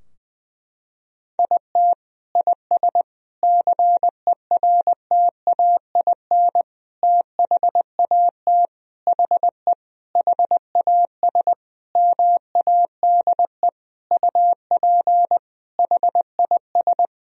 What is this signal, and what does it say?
Morse code, heard as a keyed tone, about 20 wpm, 700 Hz. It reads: IT IS CERTAIN THAT HE HAS MADE UP HIS
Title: Sentences from Top 1000 Words - Review Part 2 - 20wpm